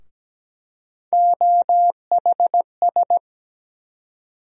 Morse code, heard as a keyed tone, about 17 wpm, 700 Hz. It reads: OHS
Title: USA Call Sign Postfixes - 3 Letters 17wpm